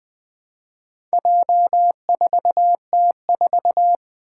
Morse code, heard as a keyed tone, about 20 wpm, 700 Hz. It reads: J4T4
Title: Four Challenging Letter-Number Combinations 20wpm